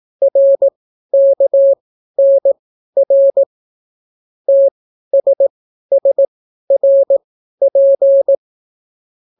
Morse code, heard as a keyed tone, about 18 wpm, spaced as for 12 wpm, 550 Hz. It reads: RKNR TSSRP